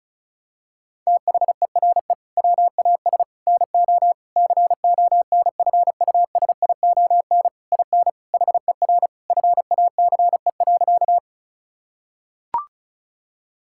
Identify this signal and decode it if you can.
Morse code, heard as a keyed tone, about 36 wpm, 700 Hz. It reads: THERE WAS NO CONFUSION IN HER FACE.
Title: Sentences from Top 1600 Words 35wpm